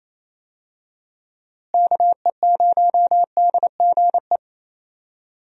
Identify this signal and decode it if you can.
Morse code, heard as a keyed tone, about 28 wpm, 700 Hz. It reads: KE0DGE
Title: USA Call Signs - 6 Characters - Repeated 2x - 28wpm